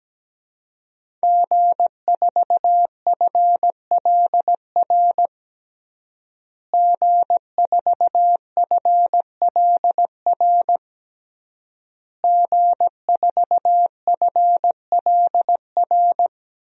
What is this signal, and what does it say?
Morse code, heard as a keyed tone, about 17 wpm, 700 Hz. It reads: G4FLR G4FLR G4FLR
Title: DXCC-100 Call Signs - 5 Characters - Repeated 3x - 17wpm